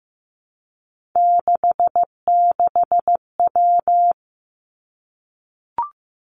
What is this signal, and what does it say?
Morse code, heard as a keyed tone, about 15 wpm, 700 Hz. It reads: 66W
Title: Three Challenging Letter-Number Combinations 15wpm